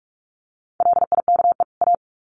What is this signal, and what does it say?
Morse code, heard as a keyed tone, about 46 wpm, 700 Hz. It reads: LIKE A
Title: Sets of 2 Words 45wpm